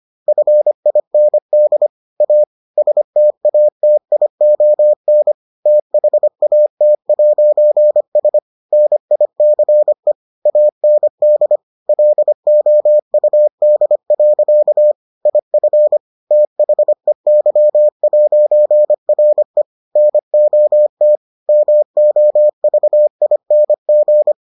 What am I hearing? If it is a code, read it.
Morse code, heard as a keyed tone, about 25 wpm, 600 Hz. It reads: FIND A STATION THAT'S NICE AND LOUD. IF THEY'RE NOT MOVING